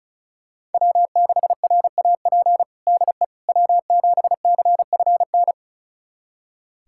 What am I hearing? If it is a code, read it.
Morse code, heard as a keyed tone, about 34 wpm, 700 Hz. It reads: W6RAP DE W7CFN
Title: USA Callsign DE Callsign - 5 Characters - 35wpm